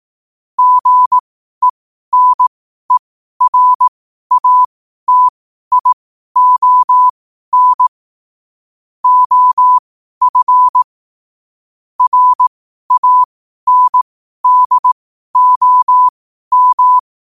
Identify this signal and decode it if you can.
Morse code, heard as a keyed tone, about 18 wpm, spaced as for 12 wpm, 1000 Hz. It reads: GENERATION OF RANDOM
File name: qotdfiles / 12wpm